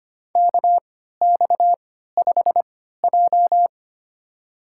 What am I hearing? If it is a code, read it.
Morse code, heard as a keyed tone, about 25 wpm, spaced as for 14 wpm, 700 Hz. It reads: KX5J
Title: USA Call Signs - 3x Character Spacing - 25wpm